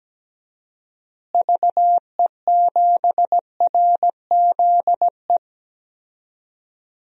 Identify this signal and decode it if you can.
Morse code, heard as a keyed tone, about 17 wpm, 700 Hz. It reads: VE7RZE
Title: DXCC-100 Call Signs - 6 Characters - Repeated 3x - 17wpm